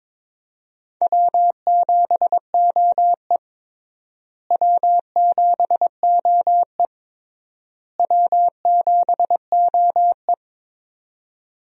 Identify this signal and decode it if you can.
Morse code, heard as a keyed tone, about 21 wpm, 700 Hz. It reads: W7OE W7OE W7OE
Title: USA Call Signs - 4 Characters - Repeated 3x - 22wpm